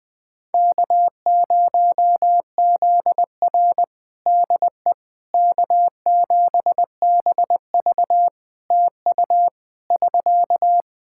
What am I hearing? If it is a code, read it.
Morse code, heard as a keyed tone, about 20 wpm, 700 Hz. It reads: K0ZR DE K7BV TU <SK>
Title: Practice QSO - Exchange 4 - Ending - 20wpm